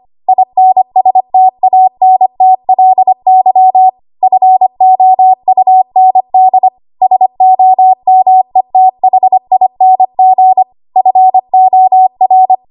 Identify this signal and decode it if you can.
Morse code, heard as a keyed tone, about 25 wpm, 750 Hz. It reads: INSTANTLY FOUND SOMETHING FOR
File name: Archive / 25 WPM / 25WPM